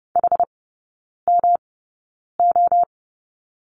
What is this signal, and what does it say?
Morse code, heard as a keyed tone, about 30 wpm, spaced as for 9 wpm, 700 Hz. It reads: HMO
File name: Lesson 18e - Letter m - Three Characters 30wpm